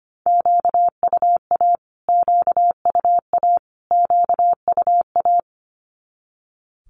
Morse code, heard as a keyed tone, about 25 wpm, 700 Hz. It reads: QUA QUA QUA
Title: Q-Codes Repeated 3x 25wpm